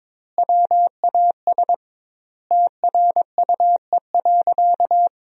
Morse code, heard as a keyed tone, about 22 wpm, 700 Hz. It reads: WAS TRUE.
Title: 2x Word-Spacing - Sentences from Top 400 Words - 22wpm